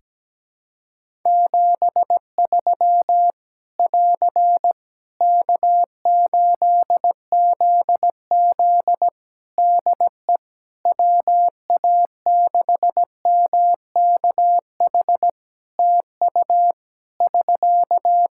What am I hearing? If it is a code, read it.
Morse code, heard as a keyed tone, about 17 wpm, 700 Hz. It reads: 73 <AR> K8ZZ DE WA6MKH TU <SK>
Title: K1USN SST QSO Practice 17wpm